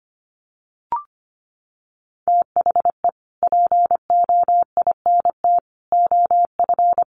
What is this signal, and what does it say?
Morse code, heard as a keyed tone, about 25 wpm, 700 Hz. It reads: THE POINT OF
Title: Sets of 5 Words 25wpm